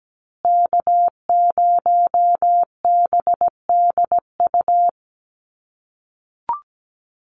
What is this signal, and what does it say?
Morse code, heard as a keyed tone, about 17 wpm, 700 Hz. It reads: K0BDU
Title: USA Call Signs Repeated 3x 17wpm